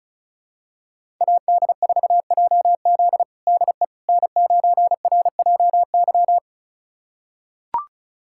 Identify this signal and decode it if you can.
Morse code, heard as a keyed tone, about 34 wpm, 700 Hz. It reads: AD4JZ DE N9RJY
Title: USA Callsign DE Callsign - 5 Characters - 35wpm